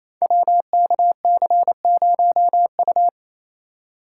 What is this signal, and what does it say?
Morse code, heard as a keyed tone, about 28 wpm, 700 Hz. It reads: WKC0U
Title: Code Groups - 5 Letter:Numbers - 28wpm